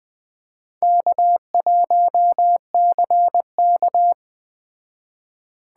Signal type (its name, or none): Morse code